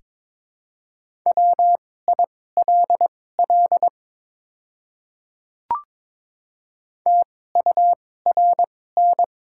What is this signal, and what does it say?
Morse code, heard as a keyed tone, about 22 wpm, spaced as for 16 wpm, 700 Hz. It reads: WILL TURN
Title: 2x Character Spacing - Top 200 Words - 22wpm